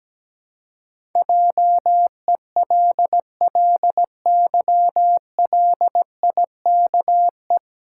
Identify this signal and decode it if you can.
Morse code, heard as a keyed tone, about 17 wpm, 700 Hz. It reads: JELLYLIKE
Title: Pangram Challenge v2 - 17wpm